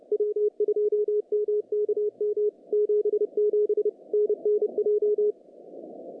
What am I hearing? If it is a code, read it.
Morse code, heard as a keyed tone, about 30 wpm, 420 Hz. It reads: W2MKM 77 NNJ
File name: contests / CW-SS-k5tr- -0000